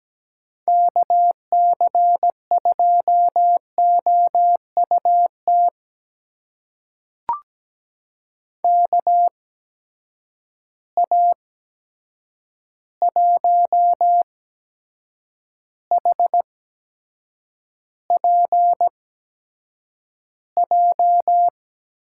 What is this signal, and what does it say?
Morse code, heard as a keyed tone, about 17 wpm, spaced as for 7 wpm, 700 Hz. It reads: KC2OUT KA1HPJ
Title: USA Call Signs - 6 Characters - 8x Character Spacing - 17wpm